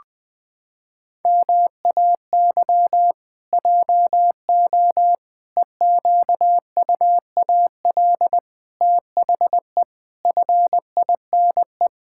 Morse code, heard as a keyed tone, about 21 wpm, 700 Hz. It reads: MAY JO EQUAL THE FINE
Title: Pangram Challenge v2 - 20wpm